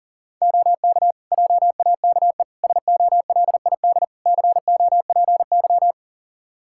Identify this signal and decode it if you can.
Morse code, heard as a keyed tone, about 40 wpm, 700 Hz. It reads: OK JAKE SOLID COPY
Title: Practice QSO - Exchange 3 - More Info - 40wpm